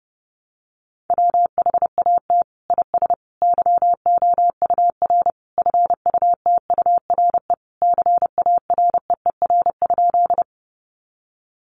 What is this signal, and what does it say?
Morse code, heard as a keyed tone, about 30 wpm, 700 Hz. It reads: WHAT IS YOUR FUTURE CAREER?
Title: Sentences from Top 1000 Words 30wpm